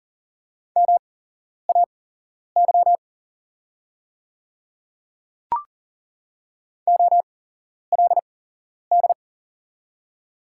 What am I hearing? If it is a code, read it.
Morse code, heard as a keyed tone, about 41 wpm, spaced as for 11 wpm, 700 Hz. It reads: MAY OLD
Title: 8x Character Spacing - Top 300 Words - 40wpm